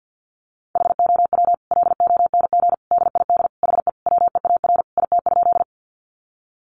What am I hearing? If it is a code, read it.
Morse code, heard as a keyed tone, about 50 wpm, 700 Hz. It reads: HOW LONG DID HE WEAR IT?
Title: Sentences from Top 1500 Words 50wpm